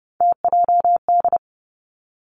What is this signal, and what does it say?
Morse code, heard as a keyed tone, about 32 wpm, 700 Hz. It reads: TJD
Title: Three Challenging Letter-Number Combinations 30wpm